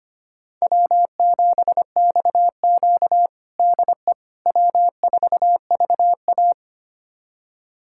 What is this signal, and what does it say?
Morse code, heard as a keyed tone, about 25 wpm, 700 Hz. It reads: W7XQ DE W4VA
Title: USA Callsign DE Callsign - 4 Characters - 25wpm